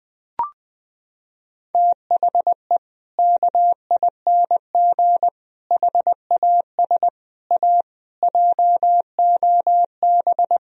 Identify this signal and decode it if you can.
Morse code, heard as a keyed tone, about 20 wpm, 700 Hz. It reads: THE KING HAS A JOB
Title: Sentences from Top 1000 Words - Review Part 3 - 20wpm